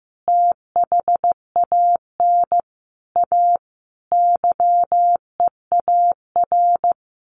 Morse code, heard as a keyed tone, about 15 wpm, 700 Hz. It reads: THAN A YEAR
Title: Sets of 5 Words 15wpm